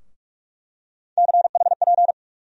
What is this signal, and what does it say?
Morse code, heard as a keyed tone, about 46 wpm, 700 Hz. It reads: CHP
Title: USA Call Sign Postfixes - 3 Letters 45wpm